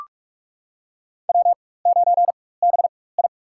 Morse code, heard as a keyed tone, about 43 wpm, spaced as for 21 wpm, 700 Hz. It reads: W9BI
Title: USA Call Signs - 4x Character Spacing - 45wpm